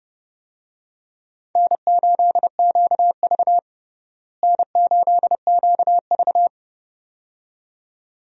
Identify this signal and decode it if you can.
Morse code, heard as a keyed tone, about 30 wpm, 700 Hz. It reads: N8QV N8QV
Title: USA Call Signs - 4 Characters - Repeated 2x - 30wpm